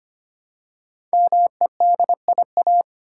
Morse code, heard as a keyed tone, about 25 wpm, 700 Hz. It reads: MEDIA
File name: 5 Letter Words - First 3 Letters Given - 25wpm